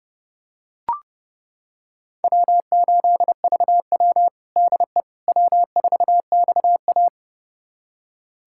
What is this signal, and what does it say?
Morse code, heard as a keyed tone, about 30 wpm, 700 Hz. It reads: W8VW DE W4XA
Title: USA Callsign DE Callsign - 4 Characters - 30wpm